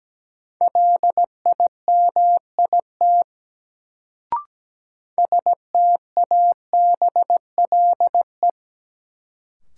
Morse code, heard as a keyed tone, about 17 wpm, 700 Hz. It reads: LIMIT STABLE
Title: Sentences from Top 1500 Words - New Word Drill Alt - 17wpm